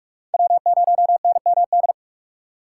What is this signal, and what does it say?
Morse code, heard as a keyed tone, about 41 wpm, 700 Hz. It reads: W0NMD